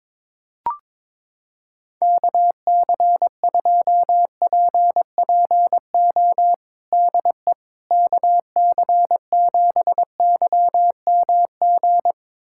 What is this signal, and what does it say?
Morse code, heard as a keyed tone, about 22 wpm, 700 Hz. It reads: KC2PPO DE KC7YMG
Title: USA Callsign DE Callsign - 6 Characters - 22wpm